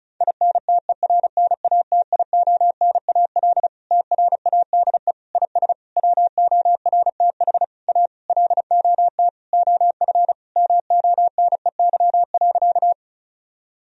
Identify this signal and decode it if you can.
Morse code, heard as a keyed tone, about 36 wpm, 700 Hz. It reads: INTERNATIONAL TRADE IS WORTH A LOT OF MONEY.
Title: Sentences from Top 1000 Words - Review Part 4 - 35wpm